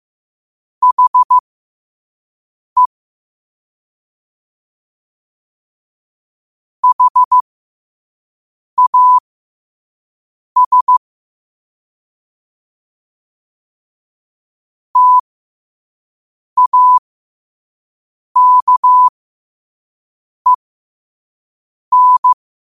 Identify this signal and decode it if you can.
Morse code, heard as a keyed tone, about 15 wpm, spaced as for 5 wpm, 1000 Hz. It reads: HE HAS TAKEN
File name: qotdfiles / 5wpm